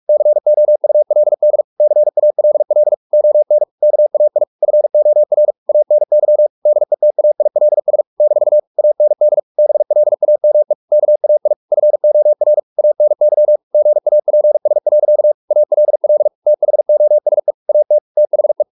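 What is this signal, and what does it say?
Morse code, heard as a keyed tone, about 44 wpm, 600 Hz. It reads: COULD CALL ON KAI FOR ANY DETAILS - AND BLAME KAI FOR ANY GAPS. ALL THOSE AT THE
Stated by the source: Morse practice at 45 wpm, the novel Never by Ken Follett in English, from CW med Gnister.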